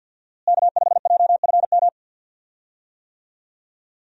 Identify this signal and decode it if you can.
Morse code, heard as a keyed tone, about 52 wpm, 700 Hz. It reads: K5JFM
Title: USA Call Signs - 5 Characters - Repeated 3x - 50wpm